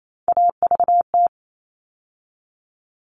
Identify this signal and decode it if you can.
Morse code, heard as a keyed tone, about 28 wpm, 700 Hz. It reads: AVT
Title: All Cut Numbers - 3-Digit 28wpm